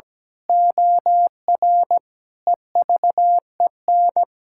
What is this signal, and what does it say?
Morse code, heard as a keyed tone, about 17 wpm, 700 Hz. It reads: OR EVEN
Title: Pangram Challenge v2 - 17wpm